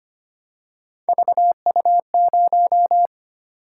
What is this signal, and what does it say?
Morse code, heard as a keyed tone, about 25 wpm, 700 Hz. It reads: VU0
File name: Three Letter-Number Combinations - 25wpm